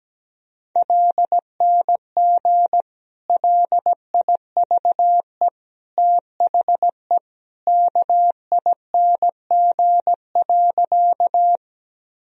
Morse code, heard as a keyed tone, about 17 wpm, 700 Hz. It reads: LNG LIVE THE KING.
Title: Sentences from Top 400 Words and Abbreviations 17wpm